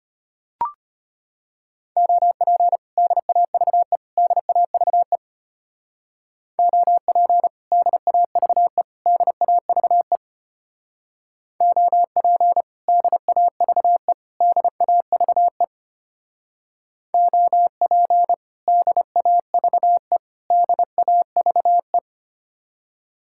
Morse code, heard as a keyed tone, about 31 wpm, 700 Hz. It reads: OP DAVE DAVE OP DAVE DAVE OP DAVE DAVE OP DAVE DAVE
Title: Speed-Racing Common QSO Phrases - 25wpm